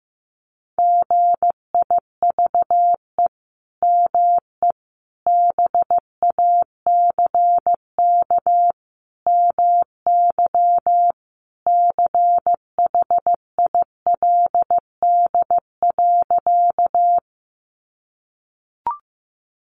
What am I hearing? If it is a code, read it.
Morse code, heard as a keyed tone, about 15 wpm, 700 Hz. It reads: GIVE ME BACK MY CHILD.
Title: Sentences from Top 800 Words 15wpm